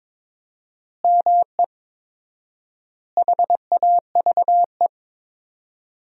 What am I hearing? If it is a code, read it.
Morse code, heard as a keyed tone, about 23 wpm, 700 Hz. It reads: ME HAVE
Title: 4x Word-Spacing - Sentences from Top 300 Words - 22wpm